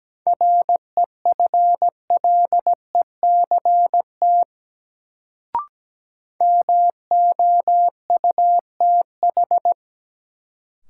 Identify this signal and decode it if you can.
Morse code, heard as a keyed tone, about 17 wpm, 700 Hz. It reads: REFLECT MOUTH
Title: Sentences from Top 1500 Words - New Word Drill - 17wpm